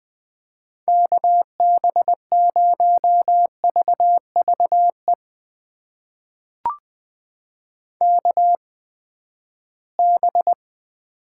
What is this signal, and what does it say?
Morse code, heard as a keyed tone, about 20 wpm, spaced as for 14 wpm, 700 Hz. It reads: KB0VVE KB
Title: USA Call Signs - 8x Character Spacing - 20wpm